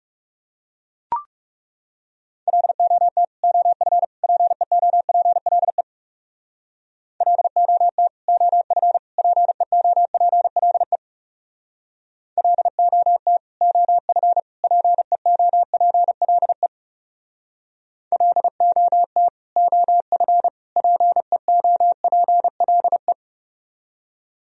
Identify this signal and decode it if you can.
Morse code, heard as a keyed tone, about 38 wpm, 700 Hz. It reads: LOT OF PEOPLE LOT OF PEOPLE LOT OF PEOPLE LOT OF PEOPLE